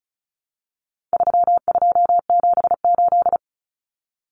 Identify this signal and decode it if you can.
Morse code, heard as a keyed tone, about 34 wpm, 700 Hz. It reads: 3278
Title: Common Cut Numbers - 4-Digit 35wpm